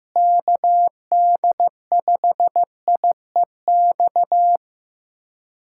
Morse code, heard as a keyed tone, about 16 wpm, 700 Hz. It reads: KD5IEX
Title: USA Callsign DE Callsign 15wpm